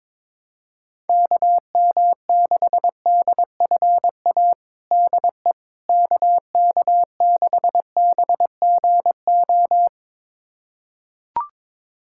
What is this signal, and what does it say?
Morse code, heard as a keyed tone, about 22 wpm, 700 Hz. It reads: KM6DFA DE KK6BGO
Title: USA Callsign DE Callsign - 6 Characters - 22wpm